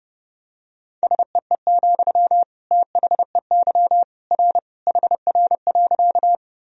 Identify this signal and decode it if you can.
Morse code, heard as a keyed tone, about 32 wpm, 700 Hz. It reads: SEE, THEY R HR.
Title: Sentences from Top 200 Words and Abbreviations 30wpm